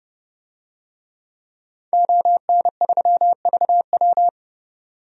Morse code, heard as a keyed tone, about 31 wpm, 700 Hz. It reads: ON3VW
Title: DXCC-100 Call Signs - 5 Characters - 30wpm